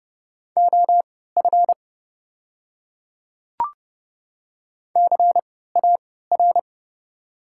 Morse code, heard as a keyed tone, about 30 wpm, spaced as for 17 wpm, 690 Hz. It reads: OF CAR
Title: 3x Character Spacing - Top 300 Words - 30wpm